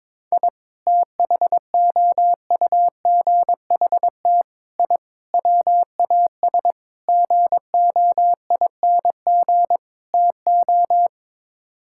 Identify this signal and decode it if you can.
Morse code, heard as a keyed tone, about 22 wpm, 700 Hz. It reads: I THOUGHT I WAS GOING TO
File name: practice / 22wpm / Sets of 6 Words - 22wpm